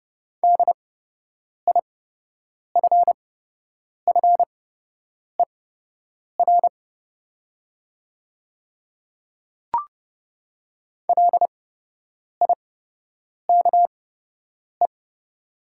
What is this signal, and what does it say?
Morse code, heard as a keyed tone, about 30 wpm, spaced as for 8 wpm, 700 Hz. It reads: DIFFER LIKE
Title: 8x Character Spacing - Top 300 Words - 30wpm